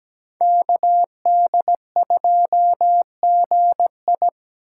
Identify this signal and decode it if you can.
Morse code, heard as a keyed tone, about 20 wpm, 700 Hz. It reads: KD2GI
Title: Speed Racing - USA 2x2 Call Signs 17wpm